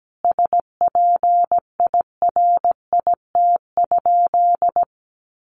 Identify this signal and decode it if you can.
Morse code, heard as a keyed tone, about 17 wpm, 700 Hz. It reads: SPIRIT?
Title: Sentences from Top 1400 Words 17wpm